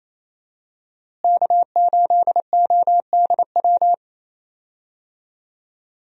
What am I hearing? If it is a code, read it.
Morse code, heard as a keyed tone, about 28 wpm, 700 Hz. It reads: K8ODW
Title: USA Call Signs - 5 Characters - Repeated 3x - 28wpm